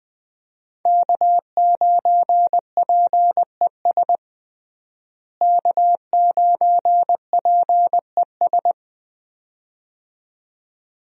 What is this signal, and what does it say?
Morse code, heard as a keyed tone, about 20 wpm, 700 Hz. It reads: K9PES K9PES
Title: USA Call Signs - 5 Characters - Repeated 2x - 20wpm